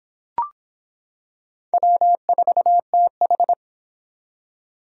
Morse code, heard as a keyed tone, about 25 wpm, 700 Hz. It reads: W4TH